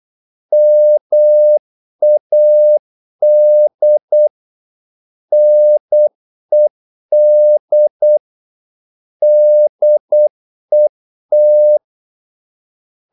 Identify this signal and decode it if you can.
Morse code, heard as a keyed tone, about 8 wpm, 600 Hz. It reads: MAD NED DET
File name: Korte ord DK 8wpm